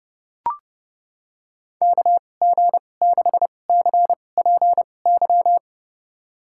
Morse code, heard as a keyed tone, about 30 wpm, spaced as for 22 wpm, 700 Hz. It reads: KG6CPY
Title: USA Call Signs - 6 Characters - 2x Character Spacing - 30wpm